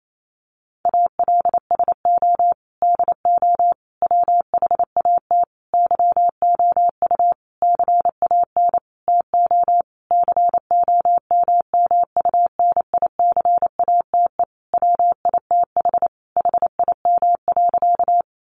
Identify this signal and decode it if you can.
Morse code, heard as a keyed tone, about 28 wpm, 700 Hz. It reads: ALSO DO WHAT YOU CAN TO COMMUNICATE WITH HIM.
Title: Sentences from Top 1300 Words 28wpm